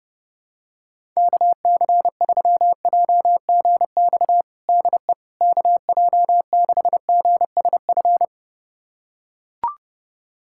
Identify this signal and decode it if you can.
Morse code, heard as a keyed tone, about 30 wpm, 700 Hz. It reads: KC3JGX DE KJ6GSF
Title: USA Callsign DE Callsign - 6 Characters - 30wpm